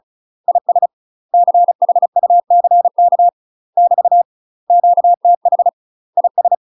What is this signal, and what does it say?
Morse code, heard as a keyed tone, about 34 wpm, 700 Hz. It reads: IS CHUCK <BT> QTH IS
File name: UK MASTER QSOs TRAINING / 35WPM / UK MASTER QSOs 0015